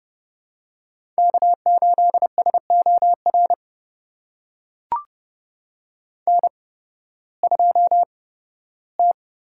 Morse code, heard as a keyed tone, about 30 wpm, spaced as for 16 wpm, 700 Hz. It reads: K8SOR N2T
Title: USA Call Signs - 5 Characters - 8x Character Spacing - 30wpm